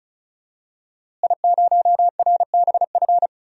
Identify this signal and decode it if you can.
Morse code, heard as a keyed tone, about 34 wpm, 700 Hz. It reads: I0RBF
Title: DXCC-100 Call Signs - 5 Characters - Repeated 2x - 35wpm